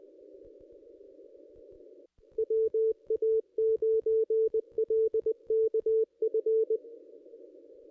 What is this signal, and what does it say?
Morse code, heard as a keyed tone, about 20 wpm, 440 Hz. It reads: WA9LKF
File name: contests / ss-cw-k5tr-op-n5rz -1400